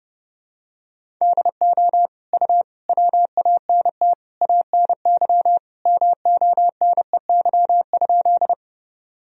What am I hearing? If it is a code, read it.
Morse code, heard as a keyed tone, about 30 wpm, 700 Hz. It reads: DO U WANT ANY MONEY?
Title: Sentences from Top 400 Words and Abbreviations 30wpm